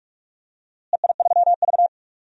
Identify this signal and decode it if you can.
Morse code, heard as a keyed tone, about 46 wpm, 700 Hz. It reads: EI3V
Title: DXCC-100 Call Signs - 4 Characters - Repeated 3x - 45wpm